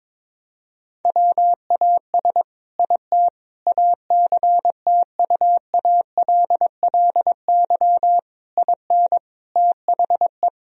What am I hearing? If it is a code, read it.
Morse code, heard as a keyed tone, about 22 wpm, 700 Hz. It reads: WAS IT ACTUALLY IN THE
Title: Sentences from Top 1000 Words - Review Part 4 - 22wpm